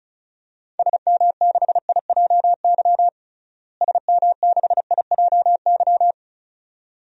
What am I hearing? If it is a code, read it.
Morse code, heard as a keyed tone, about 36 wpm, 700 Hz. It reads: SM6IJY SM6IJY
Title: DXCC-100 Call Signs - 6 Characters - Repeated 2x - 35wpm